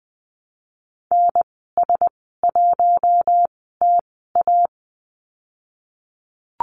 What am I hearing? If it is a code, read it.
Morse code, heard as a keyed tone, about 20 wpm, 700 Hz. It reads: N S 1 T A
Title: 055 Lesson 10g - Letter r - Call Signs 20wpm